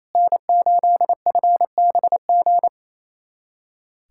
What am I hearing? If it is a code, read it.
Morse code, heard as a keyed tone, about 28 wpm, 700 Hz. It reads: N8FBG
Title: USA Callsign DE Callsign - 5 Characters - 28wpm